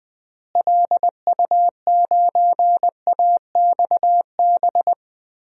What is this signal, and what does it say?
Morse code, heard as a keyed tone, about 21 wpm, 700 Hz. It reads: LU9AXB
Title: DXCC-100 Call Signs - 6 Characters - Repeated 2x - 20wpm